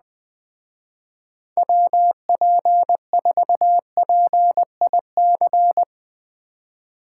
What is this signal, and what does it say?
Morse code, heard as a keyed tone, about 20 wpm, 700 Hz. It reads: WP4PIC